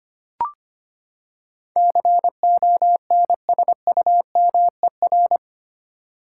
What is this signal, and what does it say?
Morse code, heard as a keyed tone, about 25 wpm, 700 Hz. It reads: CONSUMER
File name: Eight Letter Words - 25wpm